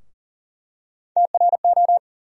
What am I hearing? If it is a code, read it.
Morse code, heard as a keyed tone, about 40 wpm, 700 Hz. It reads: TRO
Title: USA Call Sign Postfixes - 3 Letters 40wpm